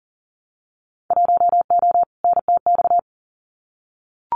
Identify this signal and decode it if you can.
Morse code, heard as a keyed tone, about 41 wpm, 700 Hz. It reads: 1O NTX
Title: Winter Field Day Exchanges 40wpm